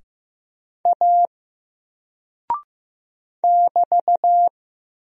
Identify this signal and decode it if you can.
Morse code, heard as a keyed tone, about 15 wpm, 700 Hz. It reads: A <BT>
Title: Single Character (Letter:Number:Punctuation:Prosign) Alt 15wpm